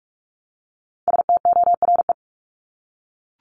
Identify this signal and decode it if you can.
Morse code, heard as a keyed tone, about 44 wpm, 700 Hz. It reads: STORE
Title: Top 501-750 Words 45wpm